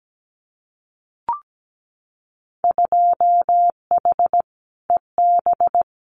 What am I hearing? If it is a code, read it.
Morse code, heard as a keyed tone, about 17 wpm, 700 Hz. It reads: 2H EB
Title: Winter Field Day Exchanges Encore 17wpm